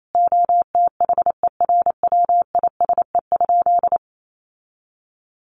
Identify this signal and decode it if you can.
Morse code, heard as a keyed tone, about 28 wpm, 700 Hz. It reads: OTHERWISE?
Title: Sentences from Top 1200 Words 28wpm